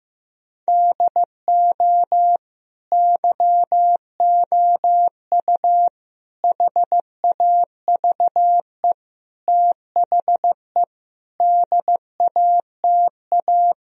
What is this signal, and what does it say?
Morse code, heard as a keyed tone, about 15 wpm, 700 Hz. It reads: DO YOU HAVE THE DATA
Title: Sentences from Top 1000 Words - Review Part 3 - 15wpm